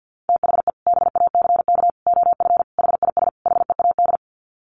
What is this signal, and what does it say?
Morse code, heard as a keyed tone, about 50 wpm, 700 Hz. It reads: THE BACK OF HIS HEAD
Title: Sets of 5 Words Encore 50wpm